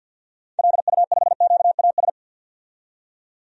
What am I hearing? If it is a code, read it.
Morse code, heard as a keyed tone, about 52 wpm, 700 Hz. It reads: LU5QAS
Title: DXCC-100 Call Signs - 6 Characters - 50wpm